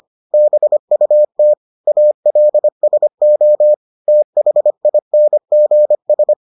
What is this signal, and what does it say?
Morse code, heard as a keyed tone, about 25 wpm, 600 Hz. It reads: BUT ALSO THINGS